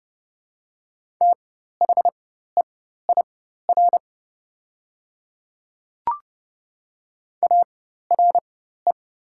Morse code, heard as a keyed tone, about 32 wpm, spaced as for 14 wpm, 690 Hz. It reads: THEIR ARE
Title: 4x Character Spacing - Top 100 Words - 30wpm